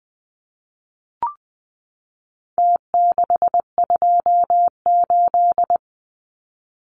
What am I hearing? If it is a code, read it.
Morse code, heard as a keyed tone, about 20 wpm, 700 Hz. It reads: T628
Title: Numbers - Contest Style - 4-Digit 20wpm